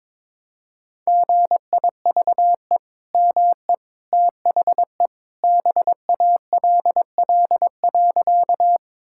Morse code, heard as a keyed tone, about 22 wpm, 700 Hz. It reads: GIVE ME THE BALL.
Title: Sentences from Top 1000 Words - Review Part 6 - 22wpm